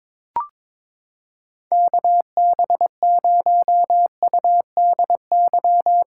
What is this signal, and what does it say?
Morse code, heard as a keyed tone, about 22 wpm, 700 Hz. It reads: KB0UDY
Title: USA Callsign DE Callsign - 6 Characters - 22wpm